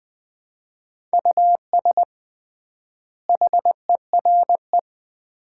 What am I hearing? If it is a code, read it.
Morse code, heard as a keyed tone, about 20 wpm, 700 Hz. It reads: US HERE
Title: 3x Word-Spacing - Sets of 5 Words - 20wpm